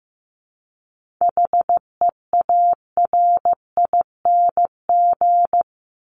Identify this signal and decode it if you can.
Morse code, heard as a keyed tone, about 15 wpm, 700 Hz. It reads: HEARING
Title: ING Suffix - Top 100 Words with Base Words 15wpm